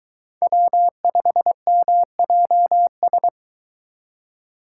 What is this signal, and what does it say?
Morse code, heard as a keyed tone, about 22 wpm, 700 Hz. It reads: W5MJS